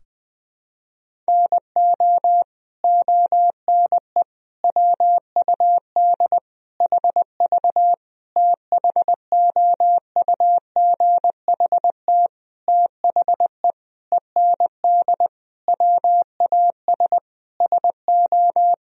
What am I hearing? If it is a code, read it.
Morse code, heard as a keyed tone, about 20 wpm, 700 Hz. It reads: NO ONE WUD HV THOUGHT THE END WAS SO
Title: Sentences from Top 300 Words and Abbreviations 20wpm